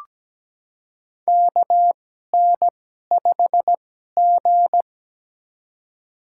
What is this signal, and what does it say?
Morse code, heard as a keyed tone, about 17 wpm, spaced as for 12 wpm, 700 Hz. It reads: KN5G
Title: USA Call Signs - 4 Characters - 2x Character Spacing - 17wpm